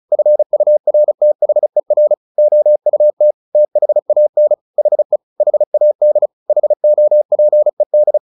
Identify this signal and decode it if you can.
Morse code, heard as a keyed tone, about 34 wpm, 600 Hz. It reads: FURTHER OUT THAN HE HAD HOPED
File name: the old man and the sea / the old man and the sea 1 35wpm